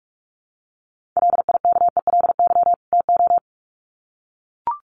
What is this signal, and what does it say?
Morse code, heard as a keyed tone, about 44 wpm, 700 Hz. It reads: LIKELY TO
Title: Sets of 2 Words 45wpm